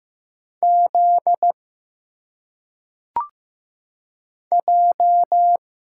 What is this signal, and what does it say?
Morse code, heard as a keyed tone, about 15 wpm, 700 Hz. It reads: Z J
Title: Single Letters - Phonetic Pronunciation 15wpm